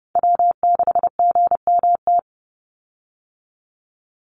Morse code, heard as a keyed tone, about 30 wpm, 700 Hz. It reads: W6GMT
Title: Call Signs - 30wpm